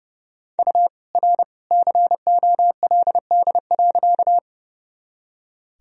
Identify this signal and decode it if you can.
Morse code, heard as a keyed tone, about 32 wpm, 700 Hz. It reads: U R COLD.
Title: Sentences from Top 400 Words and Abbreviations 30wpm